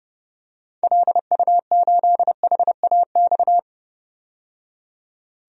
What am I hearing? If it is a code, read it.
Morse code, heard as a keyed tone, about 30 wpm, 690 Hz. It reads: LU8HAX
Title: DXCC-100 Call Signs - 6 Characters - 30wpm